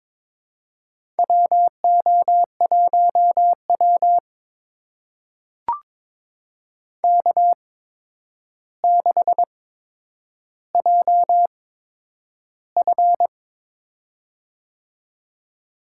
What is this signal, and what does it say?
Morse code, heard as a keyed tone, about 22 wpm, spaced as for 9 wpm, 700 Hz. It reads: WO1W K6JF
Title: USA Call Signs - 4 Characters - 8x Character Spacing - 22wpm